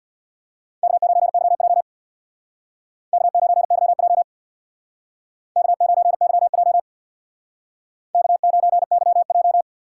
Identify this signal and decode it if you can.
Morse code, heard as a keyed tone, about 63 wpm, 700 Hz. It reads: K9YJ K9YJ K9YJ K9YJ